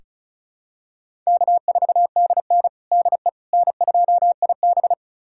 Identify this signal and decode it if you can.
Morse code, heard as a keyed tone, about 34 wpm, 700 Hz. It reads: K4DN DE N2IB
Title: USA Callsign DE Callsign - 4 Characters - 35wpm